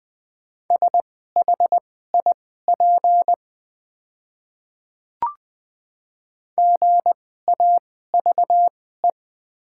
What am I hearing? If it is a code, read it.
Morse code, heard as a keyed tone, about 20 wpm, spaced as for 14 wpm, 700 Hz. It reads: SHIP GAVE
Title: 2x Character Spacing - Top 500 Words - 20wpm